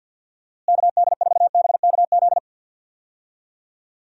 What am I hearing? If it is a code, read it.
Morse code, heard as a keyed tone, about 50 wpm, 700 Hz. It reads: KD4BKZ
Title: USA Call Signs - 6 Characters - Repeated 3x - 50wpm